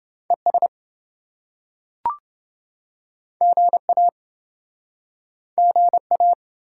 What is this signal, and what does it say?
Morse code, heard as a keyed tone, about 28 wpm, 700 Hz. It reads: ES GA GA